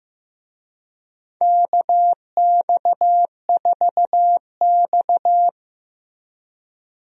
Morse code, heard as a keyed tone, about 16 wpm, 700 Hz. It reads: KX4X
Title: USA Call Signs - 4 Characters - Repeated 3x - 15wpm